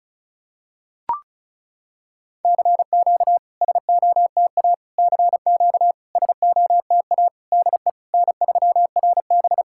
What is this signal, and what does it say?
Morse code, heard as a keyed tone, about 34 wpm, 690 Hz. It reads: CQ SOTA CQ SOTA DE N3RB
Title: SOTA QSO Practice 35wpm